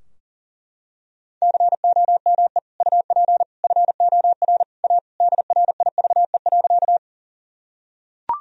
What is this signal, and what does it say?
Morse code, heard as a keyed tone, about 40 wpm, 700 Hz. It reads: COME UP FOR A DRIVE.
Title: Sentences from Top 1000 Words - Review Part 2 - 40wpm